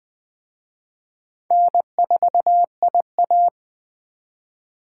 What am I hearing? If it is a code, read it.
Morse code, heard as a keyed tone, about 20 wpm, 700 Hz. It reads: N4IA
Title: Speed Racing - USA Call Signs 17wpm